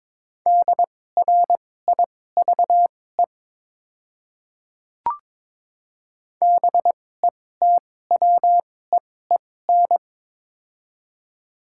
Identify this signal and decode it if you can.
Morse code, heard as a keyed tone, about 22 wpm, spaced as for 16 wpm, 690 Hz. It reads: DRIVE BETWEEN
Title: 2x Character Spacing - Top 500 Words - 22wpm